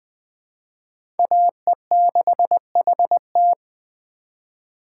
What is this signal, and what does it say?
Morse code, heard as a keyed tone, about 21 wpm, 700 Hz. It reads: AE6HT
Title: USA Call Signs - 5 Characters - 3x Character Spacing - 20wpm